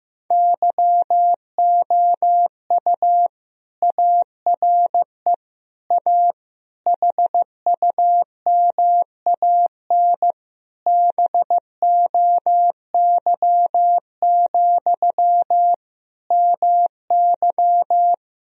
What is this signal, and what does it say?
Morse code, heard as a keyed tone, about 16 wpm, 700 Hz. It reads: YOU ARE A HUMAN BOY, MY
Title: Sentences from Top 1000 Words - Review Part 3 - 15wpm